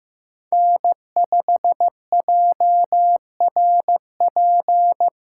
Morse code, heard as a keyed tone, about 15 wpm, 700 Hz. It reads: N5JRP
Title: USA Callsign DE Callsign - 5 Characters - 15wpm